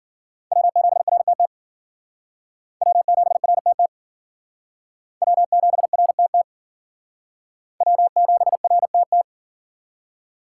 Morse code, heard as a keyed tone, about 50 wpm, 700 Hz. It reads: W7RTT W7RTT W7RTT W7RTT
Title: Speed Racing - USA 1x3 Call Signs 40wpm